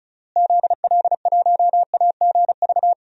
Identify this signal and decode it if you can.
Morse code, heard as a keyed tone, about 34 wpm, 700 Hz. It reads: ZL1AGV
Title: DXCC-100 Call Signs - 6 Characters - Repeated 3x - 35wpm